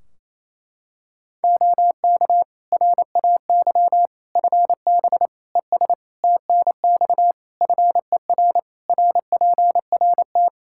Morse code, heard as a keyed tone, about 28 wpm, 700 Hz. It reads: OK RAY FB ES TNX FER RPRT
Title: Practice QSO - Exchange 2 - Info - 28wpm